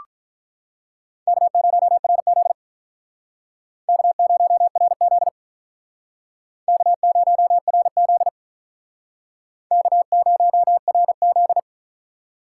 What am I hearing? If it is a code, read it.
Morse code, heard as a keyed tone, about 44 wpm, 700 Hz. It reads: K0RZ K0RZ K0RZ K0RZ